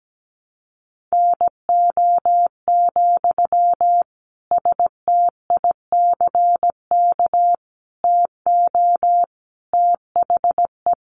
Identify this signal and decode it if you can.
Morse code, heard as a keyed tone, about 17 wpm, 700 Hz. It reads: NO, STICK TO THE
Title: Sentences from Top 1400 Words 17wpm